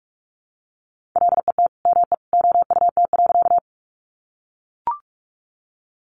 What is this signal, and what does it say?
Morse code, heard as a keyed tone, about 44 wpm, 700 Hz. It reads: LET ME OUT.
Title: Sentences from Top 300 Words 45wpm